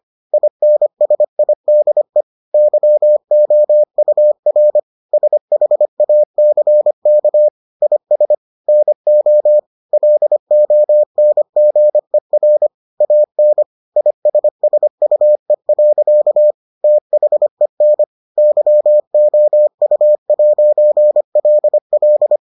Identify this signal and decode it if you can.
Morse code, heard as a keyed tone, about 25 wpm, 600 Hz. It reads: INSIDE YOUR SHACK IS NO LONGER AN ISSUE. THEN YOU'LL